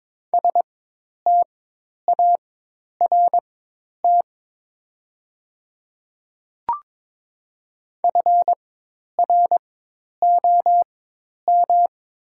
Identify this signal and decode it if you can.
Morse code, heard as a keyed tone, about 22 wpm, spaced as for 10 wpm, 700 Hz. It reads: START FROM
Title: 4x Character Spacing - Top 500 Words - 22wpm